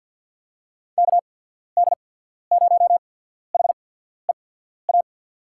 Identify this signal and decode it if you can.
Morse code, heard as a keyed tone, about 54 wpm, spaced as for 14 wpm, 700 Hz. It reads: KD0HEA
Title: USA Call Signs - 6 Characters - 8x Character Spacing - 50wpm